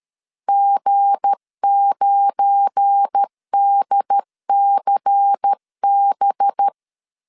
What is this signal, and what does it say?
Morse code, heard as a keyed tone, about 13 wpm, 800 Hz. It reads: G9DCB
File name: CW / DET / 1 9 1500C S B 12wpm 3819